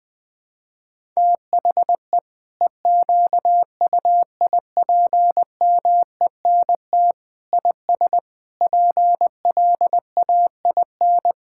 Morse code, heard as a keyed tone, about 20 wpm, 700 Hz. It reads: THE EQUIPMENT IS PLAIN
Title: Sentences from Top 1000 Words - Review Part 5 - 20wpm